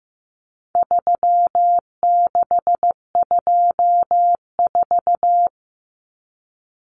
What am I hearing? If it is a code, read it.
Morse code, heard as a keyed tone, about 15 wpm, 700 Hz. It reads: 3624
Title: Common Cut Numbers - 4-Digit 15wpm